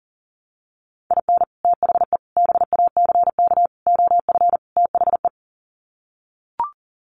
Morse code, heard as a keyed tone, about 40 wpm, 700 Hz. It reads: IN THE BACK OF THE
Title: Sets of 5 Words 40wpm